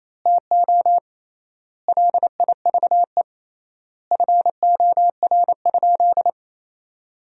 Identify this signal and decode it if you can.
Morse code, heard as a keyed tone, about 28 wpm, 700 Hz. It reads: TO LIVE FOR?
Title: 3x Word-Spacing - Sentences from Top 200 Words - 28wpm